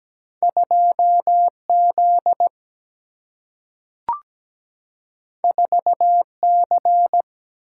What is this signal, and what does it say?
Morse code, heard as a keyed tone, about 17 wpm, 700 Hz. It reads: 2Z 4C
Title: Single-Letter Single-Number 17wpm